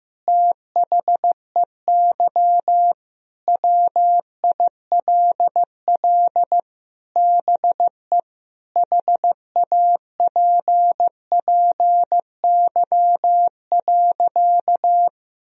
Morse code, heard as a keyed tone, about 15 wpm, 700 Hz. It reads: THEY WILL BE HAPPY.
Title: Sentences from Top 1000 Words - Review Part 2 - 15wpm